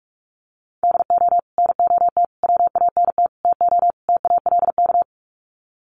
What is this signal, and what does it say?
Morse code, heard as a keyed tone, about 44 wpm, 700 Hz. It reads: DO NOT WANT TO TALK
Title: Sets of 5 Words 45wpm